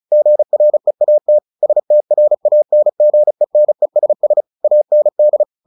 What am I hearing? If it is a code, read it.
Morse code, heard as a keyed tone, about 34 wpm, 600 Hz. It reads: GREAT STRANGENESS AND
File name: the old man and the sea 2 35wpm